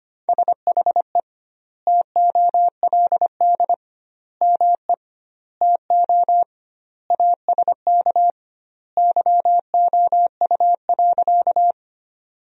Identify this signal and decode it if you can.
Morse code, heard as a keyed tone, about 25 wpm, 700 Hz. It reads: SHE TOLD ME TO ASK YOU.
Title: 2x Word-Spacing - Sentences from Top 400 Words - 25wpm